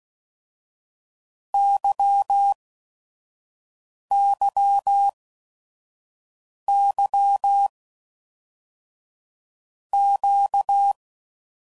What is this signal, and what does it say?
Morse code, heard as a keyed tone, about 16 wpm, 800 Hz. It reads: Y Y Y Q
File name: Basic Letter Groups By7Groups / LFCYQ / LFCYQ@5 f15